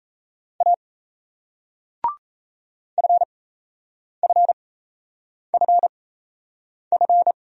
Morse code, heard as a keyed tone, about 37 wpm, spaced as for 8 wpm, 700 Hz. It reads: A FFFF